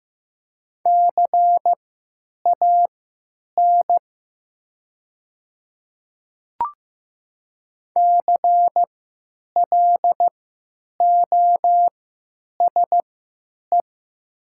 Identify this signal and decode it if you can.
Morse code, heard as a keyed tone, about 15 wpm, spaced as for 9 wpm, 700 Hz. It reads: CAN CLOSE
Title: 3x Character Spacing - Top 300 Words - 15wpm